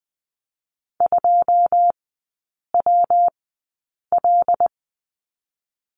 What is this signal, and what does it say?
Morse code, heard as a keyed tone, about 19 wpm, 700 Hz. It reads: 2 W L
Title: 114 Lesson 19e - Letter w - Three Characters 20wpm